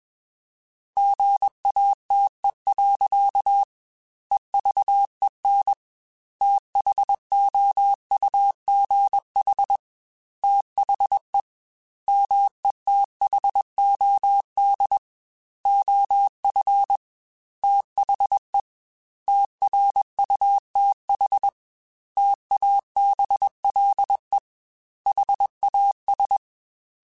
Morse code, heard as a keyed tone, about 21 wpm, 800 Hz. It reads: GATE. EVEN THOUGH THE METHOD OF THE TRUTH TABLE HAS
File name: Technical Subjects / Technical Subjects-mp3s / AND Gate @20